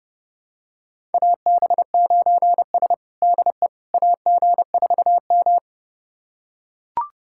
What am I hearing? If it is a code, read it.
Morse code, heard as a keyed tone, about 30 wpm, 700 Hz. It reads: AB9S DE AG4M
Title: USA Callsign DE Callsign - 4 Characters - 30wpm